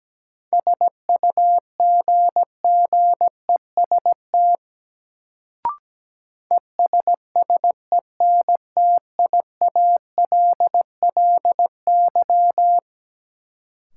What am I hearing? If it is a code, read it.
Morse code, heard as a keyed tone, about 17 wpm, 700 Hz. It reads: SUGGEST ESSENTIALLY
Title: Sentences from Top 1300 Words - New Word Drill Alt - 17wpm